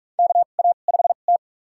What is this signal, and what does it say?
Morse code, heard as a keyed tone, about 46 wpm, spaced as for 32 wpm, 700 Hz. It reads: KA5T
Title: USA Call Signs - 4 Characters - 2x Character Spacing - 45wpm